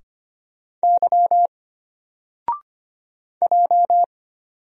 Morse code, heard as a keyed tone, about 25 wpm, 700 Hz. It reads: Y J
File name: Single-Letter-Number-25wpm